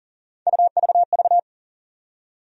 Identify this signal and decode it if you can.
Morse code, heard as a keyed tone, about 40 wpm, 700 Hz. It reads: UVV
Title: USA Call Sign Postfixes - 3 Letters 40wpm